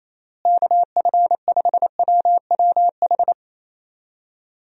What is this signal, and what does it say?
Morse code, heard as a keyed tone, about 28 wpm, 700 Hz. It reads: KF5WWH
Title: USA Call Signs - 6 Characters - Repeated 2x - 28wpm